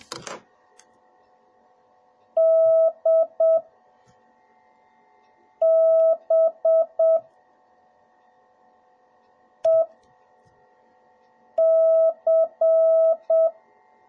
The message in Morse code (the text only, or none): DBEC